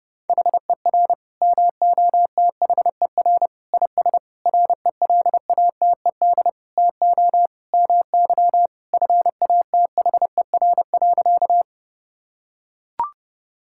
Morse code, heard as a keyed tone, about 32 wpm, 700 Hz. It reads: HER MOTHER IS RELATED TO MY FATHER.
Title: Sentences from Top 1000 Words - Review Part 5 - 30wpm